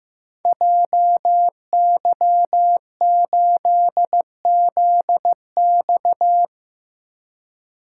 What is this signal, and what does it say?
Morse code, heard as a keyed tone, about 15 wpm, 700 Hz. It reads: JY8ZX
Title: Code Groups - 5 Letter:Numbers - 15wpm